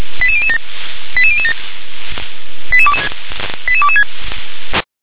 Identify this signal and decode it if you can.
Five-tone selective call (ZVEI), heard as quick runs of five tones, tones of about 70 ms. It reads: 88007 88007 88226